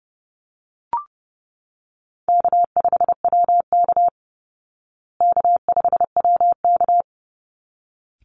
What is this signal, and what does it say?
Morse code, heard as a keyed tone, about 30 wpm, 700 Hz. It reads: K5WK K5WK
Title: Easy Call Signs Repeated 2x - 30wpm